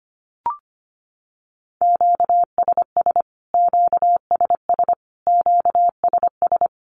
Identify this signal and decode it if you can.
Morse code, heard as a keyed tone, about 25 wpm, 700 Hz. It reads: QSS QSS QSS
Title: Q-Codes Repeated 3x 25wpm